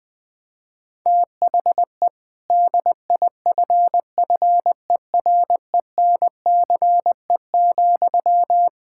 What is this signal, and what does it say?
Morse code, heard as a keyed tone, about 20 wpm, 700 Hz. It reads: THE DIFFERENCE,
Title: Sentences from Top 1000 Words - Review Part 5 - 20wpm